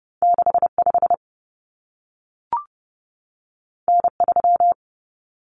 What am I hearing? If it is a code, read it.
Morse code, heard as a keyed tone, about 30 wpm, 700 Hz. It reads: 65 N3
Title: Common Cut Numbers - 2-Digit 30wpm